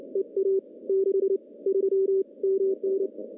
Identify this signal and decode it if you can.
Morse code, heard as a keyed tone, about 30 wpm, spaced as for 24 wpm, 390 Hz. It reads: EA 63 MN
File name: contests / CW-SS-k5tr- -1400